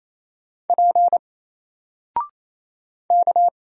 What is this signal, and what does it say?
Morse code, heard as a keyed tone, about 27 wpm, 700 Hz. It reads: P K
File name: Single-Letter-Number-28wpm